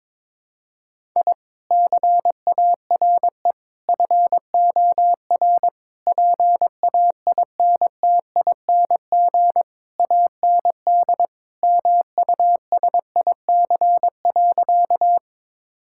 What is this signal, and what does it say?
Morse code, heard as a keyed tone, about 22 wpm, 700 Hz. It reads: I CARE FOR PAINTING AND MUSIC.
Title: Sentences from Top 1000 Words - Review Part 6 - 22wpm